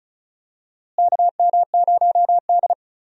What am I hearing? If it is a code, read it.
Morse code, heard as a keyed tone, about 36 wpm, 700 Hz. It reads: KM0D
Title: USA Call Signs - 4 Characters - Repeated 3x - 35wpm